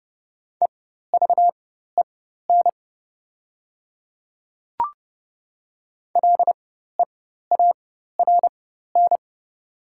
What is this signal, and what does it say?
Morse code, heard as a keyed tone, about 30 wpm, spaced as for 14 wpm, 700 Hz. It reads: EVEN LEARN
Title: 4x Character Spacing - Top 300 Words - 30wpm